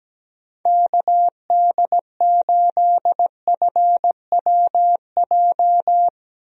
Morse code, heard as a keyed tone, about 17 wpm, 700 Hz. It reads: KD8FWJ
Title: USA Call Signs - 6 Characters - Repeated 3x - 17wpm